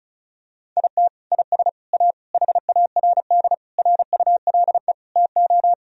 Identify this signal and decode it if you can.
Morse code, heard as a keyed tone, about 34 wpm, 700 Hz. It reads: IT IS A HARD RULE TO
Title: Sentences from Top 1000 Words - Review Part 2 - 35wpm